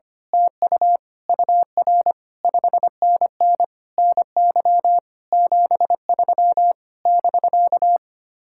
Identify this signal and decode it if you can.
Morse code, heard as a keyed tone, about 25 wpm, 700 Hz. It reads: TU UR 5NN NY 73 <BK>
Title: Speed-Racing Common QSO Phrases - 25wpm